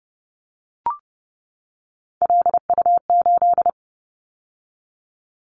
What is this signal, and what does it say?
Morse code, heard as a keyed tone, about 30 wpm, 700 Hz. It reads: LU8
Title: Callsign Prefixes for Top 100 DXCC - 30wpm